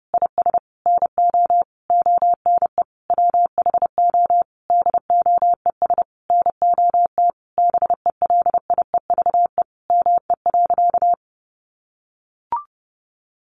Morse code, heard as a keyed tone, about 30 wpm, 700 Hz. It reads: IS NO ONE WHO DOES NOT BELIEVE ME.
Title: Sentences from Top 1000 Words 30wpm